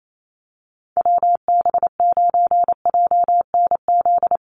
Morse code, heard as a keyed tone, about 28 wpm, 700 Hz. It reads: WB9JNZ
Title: USA Call Signs Repeated 2x 28wpm